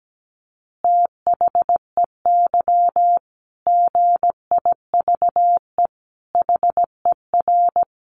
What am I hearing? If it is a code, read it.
Morse code, heard as a keyed tone, about 17 wpm, 700 Hz. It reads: THEY GIVE HER
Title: Easy Phrases 17wpm